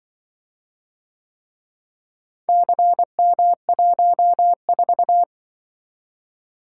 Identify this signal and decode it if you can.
Morse code, heard as a keyed tone, about 25 wpm, 700 Hz. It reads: CM14